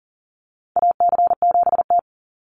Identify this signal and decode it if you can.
Morse code, heard as a keyed tone, about 40 wpm, 700 Hz. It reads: AC7T
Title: Call Signs - 40wpm